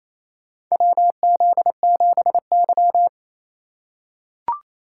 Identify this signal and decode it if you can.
Morse code, heard as a keyed tone, about 28 wpm, 700 Hz. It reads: WZ7Y
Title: USA Call Signs - 4 Characters - 4x Character Spacing - 28wpm